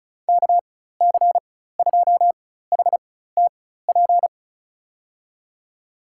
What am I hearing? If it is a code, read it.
Morse code, heard as a keyed tone, about 36 wpm, spaced as for 16 wpm, 700 Hz. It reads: KC2HTP
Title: USA Call Signs - 6 Characters - 4x Character Spacing - 35wpm